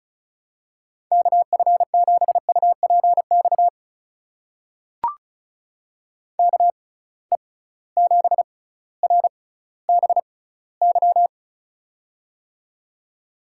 Morse code, heard as a keyed tone, about 34 wpm, spaced as for 18 wpm, 700 Hz. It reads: KF7UPX KE7RBY
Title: USA Call Signs - 6 Characters - 6x Character Spacing - 35wpm